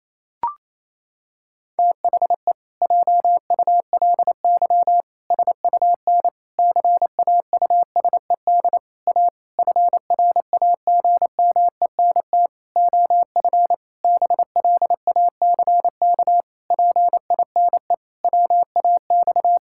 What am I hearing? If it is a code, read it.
Morse code, heard as a keyed tone, about 28 wpm, 700 Hz. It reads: THE JULY SUN CAUSED A FRAGMENT OF BLACK PINE WAX
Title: Pangram Challenge v2 - 28wpm